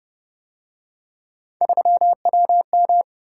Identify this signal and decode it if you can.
Morse code, heard as a keyed tone, about 29 wpm, 700 Hz. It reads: 3WM